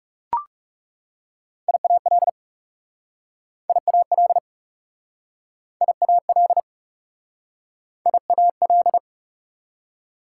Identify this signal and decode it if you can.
Morse code, heard as a keyed tone, about 37 wpm, 700 Hz. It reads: IAL IAL IAL IAL
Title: Speed-Racing Prefixes and Suffixes 30wpm